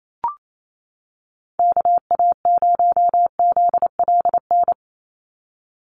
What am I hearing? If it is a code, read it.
Morse code, heard as a keyed tone, about 28 wpm, 700 Hz. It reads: KA0ZLN
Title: USA Call Signs 28wpm